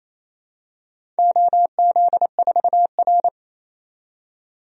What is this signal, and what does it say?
Morse code, heard as a keyed tone, about 29 wpm, 700 Hz. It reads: OZ4R
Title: DXCC-100 Call Signs - 4 Characters - 28wpm